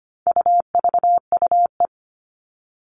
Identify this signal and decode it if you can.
Morse code, heard as a keyed tone, about 25 wpm, 700 Hz. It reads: UVUE
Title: All Cut Numbers - 4-Digit 25wpm